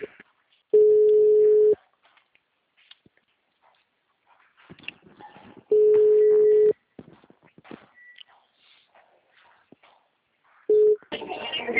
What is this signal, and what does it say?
Ringback tone (Europe, 425 Hz), with speech.
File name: out- -2027- - - .24636